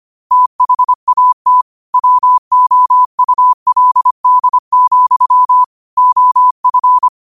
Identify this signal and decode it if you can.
Morse code, heard as a keyed tone, about 25 wpm, 1000 Hz. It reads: THAT WOULD, OF
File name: qotdfiles / 25wpm